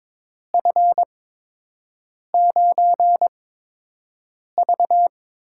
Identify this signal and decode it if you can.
Morse code, heard as a keyed tone, about 22 wpm, spaced as for 6 wpm, 700 Hz. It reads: F9V
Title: USA Call Signs - 4 Characters - 8x Character Spacing - 22wpm